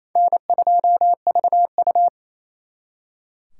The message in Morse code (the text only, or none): N2VU